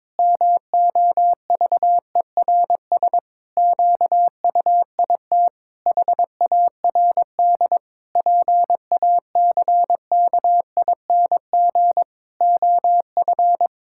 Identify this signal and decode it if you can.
Morse code, heard as a keyed tone, about 22 wpm, 700 Hz. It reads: MOVERS QUIT HARD PACKING OF
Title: Pangram Challenge v2 - 22wpm